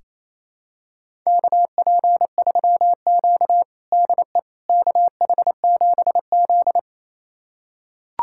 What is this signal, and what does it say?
Morse code, heard as a keyed tone, about 28 wpm, 700 Hz. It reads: KP3Q DE KH7Z
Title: USA Callsign DE Callsign - 4 Characters - 28wpm